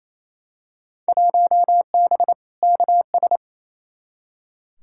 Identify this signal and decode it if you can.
Morse code, heard as a keyed tone, about 28 wpm, 700 Hz. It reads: 1B KS
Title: Field Day Exchanges Encore - 28wpm